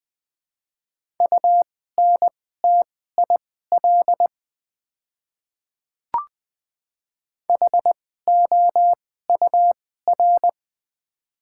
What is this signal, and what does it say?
Morse code, heard as a keyed tone, about 20 wpm, spaced as for 14 wpm, 700 Hz. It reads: UNTIL HOUR
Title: 2x Character Spacing - Top 500 Words - 20wpm